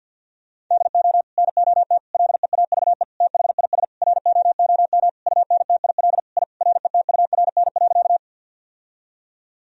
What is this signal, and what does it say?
Morse code, heard as a keyed tone, about 51 wpm, 700 Hz. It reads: DO NOT LEAVE THIS ROOM UNTIL I RETURN.
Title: Sentences from Top 1000 Words - Review Part 1 - 50wpm